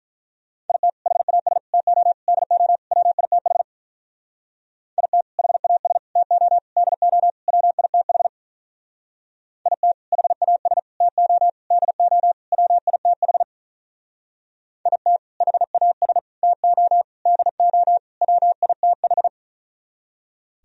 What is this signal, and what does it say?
Morse code, heard as a keyed tone, about 44 wpm, 700 Hz. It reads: IT HAS TO DO WITH IT HAS TO DO WITH IT HAS TO DO WITH IT HAS TO DO WITH